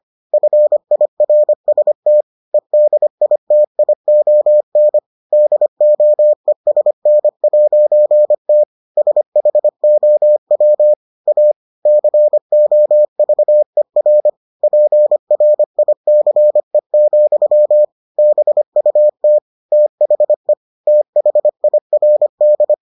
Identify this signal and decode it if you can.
Morse code, heard as a keyed tone, about 25 wpm, 600 Hz. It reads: FIRST EDITION DOESN'T SHOW A COVER PRICE, BUT THE THIRD